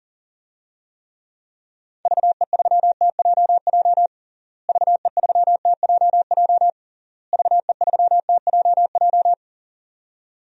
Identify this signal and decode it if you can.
Morse code, heard as a keyed tone, about 40 wpm, 700 Hz. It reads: VE3TJJ VE3TJJ VE3TJJ
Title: DXCC-100 Call Signs - 6 Characters - Repeated 3x - 40wpm